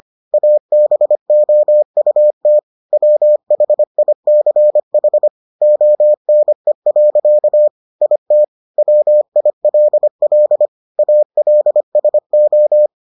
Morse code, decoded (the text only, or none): ABOUT WHICH ONE. IT WILL ALSO